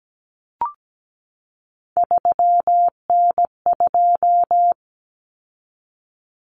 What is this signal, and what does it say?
Morse code, heard as a keyed tone, about 17 wpm, 700 Hz. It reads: 3N2
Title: Common Cut Numbers - 3-Digit 17wpm